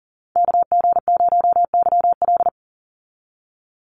Morse code, heard as a keyed tone, about 40 wpm, 700 Hz. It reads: KG0YL
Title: Call Signs Repeated 2x - 40wpm